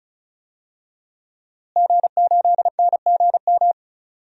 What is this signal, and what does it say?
Morse code, heard as a keyed tone, about 36 wpm, 700 Hz. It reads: G8NGM
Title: DXCC-100 Call Signs - 5 Characters - Repeated 3x - 35wpm